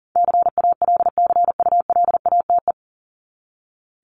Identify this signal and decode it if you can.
Morse code, heard as a keyed tone, about 40 wpm, 700 Hz. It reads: CALCULATE
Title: Top 1751-2000 Words 40wpm